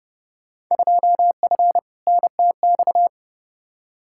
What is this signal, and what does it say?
Morse code, heard as a keyed tone, about 32 wpm, 700 Hz. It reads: 2F NTX
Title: Field Day Exchanges - 30wpm